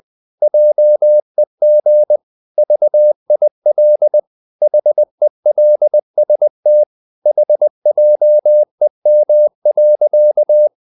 Morse code, heard as a keyed tone, about 20 wpm, 600 Hz. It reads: JEG VIL HELST HJEM.
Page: CW med Gnister